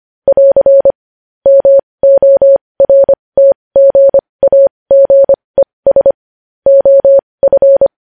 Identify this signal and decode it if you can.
Morse code, heard as a keyed tone, about 25 wpm, spaced as for 20 wpm, 550 Hz. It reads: <AR> MORTGAGES OF
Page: news headlines in morse code at 20 wpm